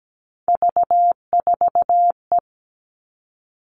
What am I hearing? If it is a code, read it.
Morse code, heard as a keyed tone, about 17 wpm, 700 Hz. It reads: V4E
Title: Three Challenging Letter-Number Combinations 17wpm